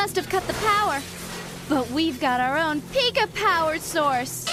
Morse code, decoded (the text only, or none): none